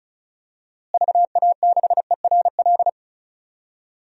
Morse code, heard as a keyed tone, about 36 wpm, 700 Hz. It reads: VA6ERL
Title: DXCC-100 Call Signs - 6 Characters - Repeated 2x - 35wpm